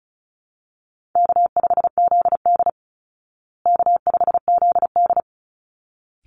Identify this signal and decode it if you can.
Morse code, heard as a keyed tone, about 34 wpm, 700 Hz. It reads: K5ZD K5ZD
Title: Easy Call Signs Repeated 2x - 35wpm